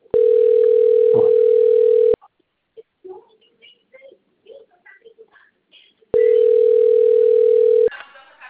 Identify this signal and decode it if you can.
Ringback tone (North America), 440 + 480 Hz.